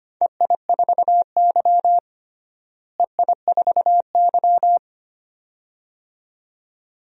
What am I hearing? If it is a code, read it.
Morse code, heard as a keyed tone, about 25 wpm, 700 Hz. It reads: EI4Y EI4Y
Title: DXCC-100 Call Signs - 4 Characters - Repeated 2x - 25wpm